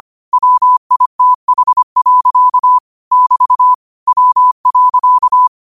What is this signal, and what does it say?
Morse code, heard as a keyed tone, about 25 wpm, 1000 Hz. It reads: WITH. <BT> W.